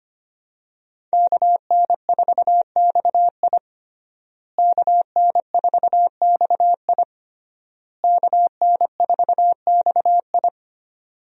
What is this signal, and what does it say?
Morse code, heard as a keyed tone, about 25 wpm, 700 Hz. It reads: KN4XI KN4XI KN4XI
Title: USA Call Signs - 5 Characters - Repeated 3x - 25wpm